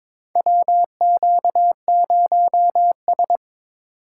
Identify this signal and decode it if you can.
Morse code, heard as a keyed tone, about 22 wpm, 700 Hz. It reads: WQ0S
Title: USA Call Signs - 8x Character Spacing - 22wpm